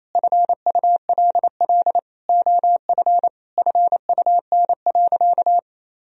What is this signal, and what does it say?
Morse code, heard as a keyed tone, about 28 wpm, 700 Hz. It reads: FULL OF FUN.
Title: Sentences from Top 1000 Words - Review Part 4 - 28wpm